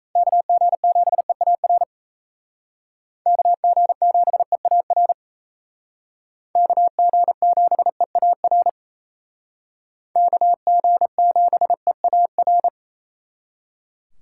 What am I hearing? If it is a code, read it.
Morse code, heard as a keyed tone, about 34 wpm, 700 Hz. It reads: KG7EAR KG7EAR KG7EAR KG7EAR
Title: Speed Racing - USA 2x3 Call Signs 28wpm